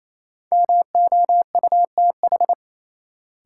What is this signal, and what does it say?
Morse code, heard as a keyed tone, about 28 wpm, 700 Hz. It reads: MOUTH
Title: Sentences from Top 1500 Words - New Word Drill - 28wpm